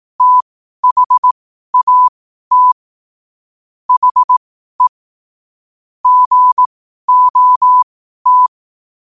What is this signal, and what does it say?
Morse code, heard as a keyed tone, about 18 wpm, spaced as for 12 wpm, 1000 Hz. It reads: THAT HE GOT
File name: qotdfiles / 12wpm